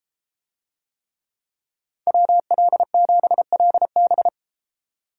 Morse code, heard as a keyed tone, about 33 wpm, 700 Hz. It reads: WL7LB